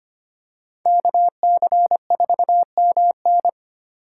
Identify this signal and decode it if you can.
Morse code, heard as a keyed tone, about 25 wpm, 700 Hz. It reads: KC4MN
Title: USA Call Signs - 5 Characters - 2x Character Spacing - 25wpm